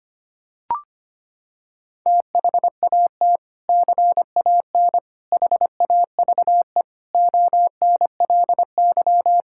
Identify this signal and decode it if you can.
Morse code, heard as a keyed tone, about 25 wpm, 700 Hz. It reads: THAT CAN HAVE ONLY
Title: Sentences from Top 1000 Words - Review Part 6 - 25wpm